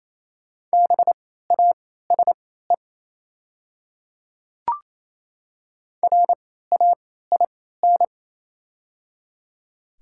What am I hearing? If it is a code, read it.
Morse code, heard as a keyed tone, about 28 wpm, spaced as for 16 wpm, 700 Hz. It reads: BASE RAIN
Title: 3x Character Spacing - Top 500 Words - 28wpm